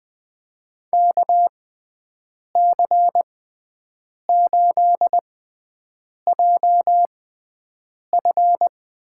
Morse code, heard as a keyed tone, about 20 wpm, spaced as for 7 wpm, 700 Hz. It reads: KC8JF
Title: USA Call Signs - 6 Characters - 6x Character Spacing - 20wpm